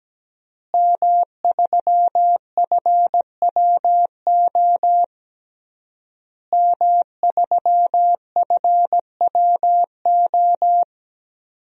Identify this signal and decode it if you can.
Morse code, heard as a keyed tone, about 17 wpm, 700 Hz. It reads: M3FWO M3FWO
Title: DXCC-100 Call Signs - 5 Characters - Repeated 3x - 17wpm